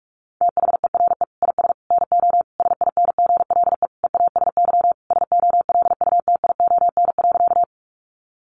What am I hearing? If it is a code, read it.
Morse code, heard as a keyed tone, about 44 wpm, 700 Hz. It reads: THERE IS NO SINGLE EASY SOLUTION.
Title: Sentences from Top 1300 Words 45wpm